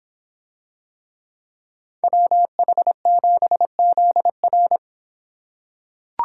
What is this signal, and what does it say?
Morse code, heard as a keyed tone, about 26 wpm, 700 Hz. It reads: WH7ZR